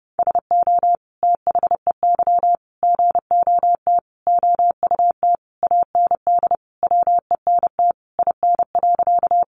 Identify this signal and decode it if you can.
Morse code, heard as a keyed tone, about 30 wpm, 700 Hz. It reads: SO THEY GOT OUT AND WENT IN.
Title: Sentences from Top 300 Words 30wpm